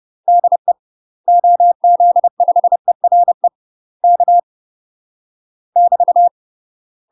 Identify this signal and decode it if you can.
Morse code, heard as a keyed tone, about 30 wpm, 700 Hz. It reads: DE OZ5ERE K <BT>
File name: UK MASTER QSOs TRAINING / 30WPM / UK MASTER QSOs 0007